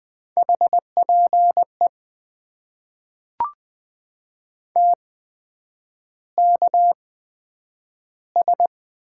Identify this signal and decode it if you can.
Morse code, heard as a keyed tone, about 20 wpm, spaced as for 9 wpm, 700 Hz. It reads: HPE TKS
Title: Top 90 QSO Elements - 8x Character Spacing - 20wpm